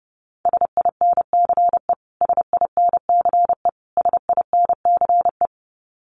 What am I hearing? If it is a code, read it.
Morse code, heard as a keyed tone, about 30 wpm, 700 Hz. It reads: SINCE SINCE SINCE
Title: Top 500 Words Repeated 30wpm